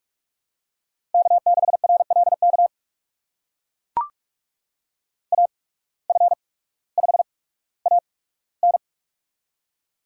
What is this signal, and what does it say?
Morse code, heard as a keyed tone, about 44 wpm, spaced as for 19 wpm, 700 Hz. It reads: K6RLK AF5AN
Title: USA Call Signs - 5 Characters - 8x Character Spacing - 45wpm